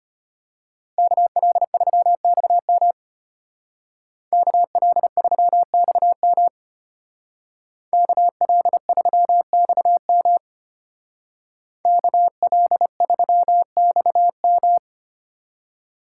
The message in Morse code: KL3XM KL3XM KL3XM KL3XM

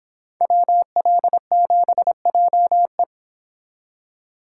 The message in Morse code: WL7JE

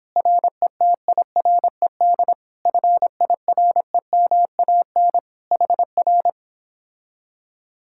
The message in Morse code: RETIRED FIREMAN HR

